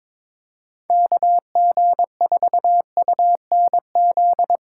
Morse code, heard as a keyed tone, about 26 wpm, 700 Hz. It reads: KG4UNZ
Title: Speed Racing - USA Call Signs 22wpm